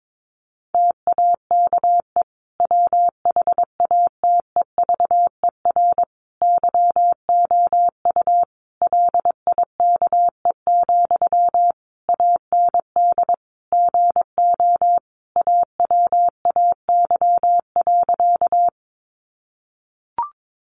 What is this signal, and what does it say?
Morse code, heard as a keyed tone, about 22 wpm, 700 Hz. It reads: TAKE WHATEVER YOU LIKE, AND GO AWAY.
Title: Sentences from Top 1100 Words 22wpm